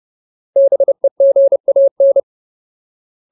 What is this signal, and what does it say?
Morse code, heard as a keyed tone, about 30 wpm, 550 Hz. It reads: BEGAN
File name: Ultimate Word List / 30WPM Ultimate